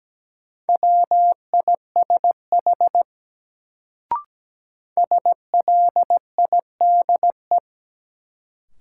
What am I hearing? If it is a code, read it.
Morse code, heard as a keyed tone, about 17 wpm, 700 Hz. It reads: WISH SLIDE
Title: Sentences from Top 1600 Words - New Word Drill - 17wpm